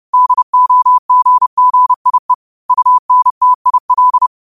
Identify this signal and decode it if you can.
Morse code, heard as a keyed tone, about 30 wpm, 1000 Hz. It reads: DOGGIE UNTIL